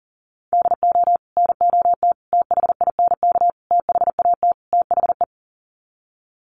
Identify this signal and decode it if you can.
Morse code, heard as a keyed tone, about 37 wpm, 700 Hz. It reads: DO NOT THINK THAT THE
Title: Sets of 5 Words Encore2 40wpm